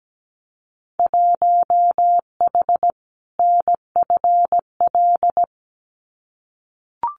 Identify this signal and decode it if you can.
Morse code, heard as a keyed tone, about 17 wpm, 700 Hz. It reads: 1H NFL
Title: Winter Field Day Exchanges 17wpm